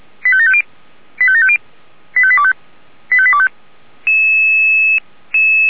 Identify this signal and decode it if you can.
Five-tone selective call (ZVEI), heard as quick runs of five tones, tones of about 70 ms. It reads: 85750 85750 85825 85825